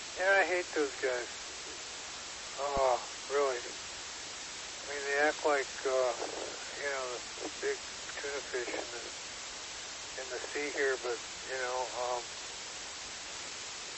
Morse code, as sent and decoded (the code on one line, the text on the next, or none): none